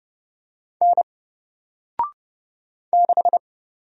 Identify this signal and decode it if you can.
Morse code, heard as a keyed tone, about 30 wpm, 700 Hz. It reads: N 6
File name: Single-Letter-Number-30wpm